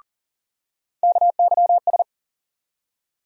Morse code, heard as a keyed tone, about 40 wpm, 700 Hz. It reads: KYS
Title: USA Call Sign Postfixes - 3 Letters 40wpm